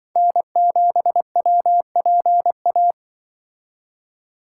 Morse code, heard as a keyed tone, about 24 wpm, 700 Hz. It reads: N7WPA
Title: Speed Racing - USA Call Signs 20wpm